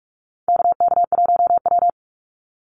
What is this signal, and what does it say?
Morse code, heard as a keyed tone, about 44 wpm, 700 Hz. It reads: KK1W
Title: NA SOTA Activator Callsigns 45wpm